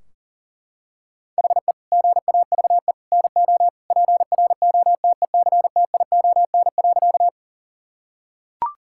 Morse code, heard as a keyed tone, about 40 wpm, 700 Hz. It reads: HE GAVE NO PROTECTION.
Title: Sentences from Top 1600 Words 40wpm